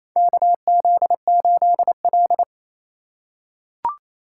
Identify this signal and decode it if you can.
Morse code, heard as a keyed tone, about 28 wpm, 700 Hz. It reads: KZ8L
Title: USA Call Signs - 4 Characters - 3x Character Spacing - 28wpm